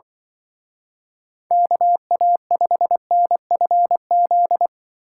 Morse code, heard as a keyed tone, about 24 wpm, 700 Hz. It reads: KA5NFZ
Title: Speed Racing - USA 2x3 Call Signs 20wpm